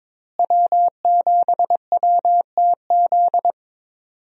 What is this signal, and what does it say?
Morse code, heard as a keyed tone, about 22 wpm, 700 Hz. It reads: W7WTZ
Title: USA Call Signs - 5 Characters - Repeated 3x - 22wpm